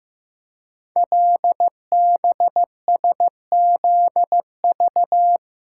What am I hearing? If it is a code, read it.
Morse code, heard as a keyed tone, about 15 wpm, 700 Hz. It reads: LBSZV
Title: Code Groups - 5 Letters - 15wpm